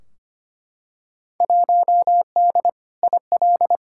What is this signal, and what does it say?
Morse code, heard as a keyed tone, about 25 wpm, 700 Hz. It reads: 1D IL
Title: Field Day Exchanges - 25wpm